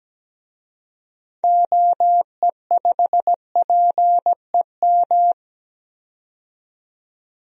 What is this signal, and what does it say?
Morse code, heard as a keyed tone, about 17 wpm, 700 Hz. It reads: OE5PEM
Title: DXCC-100 Call Signs - 6 Characters - 17wpm